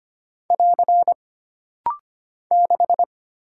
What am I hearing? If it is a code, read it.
Morse code, heard as a keyed tone, about 25 wpm, 700 Hz. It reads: <AR> 6
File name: Single Character (Letter:Number:Punctuation:Prosign) - Rapid-Fire - 25wpm